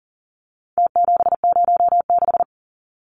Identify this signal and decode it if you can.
Morse code, heard as a keyed tone, about 40 wpm, 700 Hz. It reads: T706
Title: Numbers - Contest Style - 4-Digit 40wpm